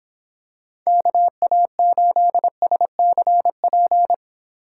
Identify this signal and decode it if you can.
Morse code, heard as a keyed tone, about 22 wpm, 700 Hz. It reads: KA8SCP